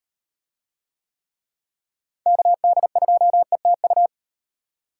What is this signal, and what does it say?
Morse code, heard as a keyed tone, about 36 wpm, 700 Hz. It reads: KD2ETU